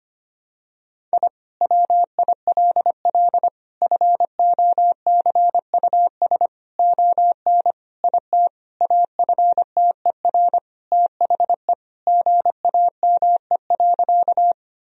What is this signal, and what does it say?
Morse code, heard as a keyed tone, about 25 wpm, 700 Hz. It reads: I WILL FOCUS ON IT AFTER THE GAME.
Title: Sentences from Top 1000 Words - Review Part 4 - 25wpm